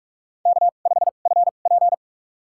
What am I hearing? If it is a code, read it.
Morse code, heard as a keyed tone, about 44 wpm, spaced as for 32 wpm, 690 Hz. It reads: K5FP
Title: USA Call Signs - 4 Characters - 2x Character Spacing - 45wpm